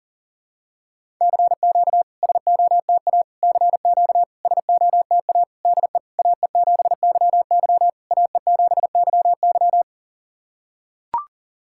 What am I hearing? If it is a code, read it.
Morse code, heard as a keyed tone, about 40 wpm, 690 Hz. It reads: CQ SOTA CQ SOTA DE AE7YY AE7YY
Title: SOTA QSO Practice 40wpm